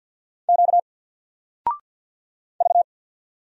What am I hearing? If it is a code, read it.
Morse code, heard as a keyed tone, about 50 wpm, 700 Hz. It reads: Q V
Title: Single Letter-Number 50wpm